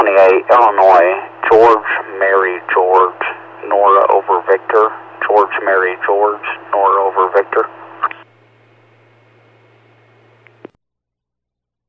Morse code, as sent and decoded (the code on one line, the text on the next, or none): none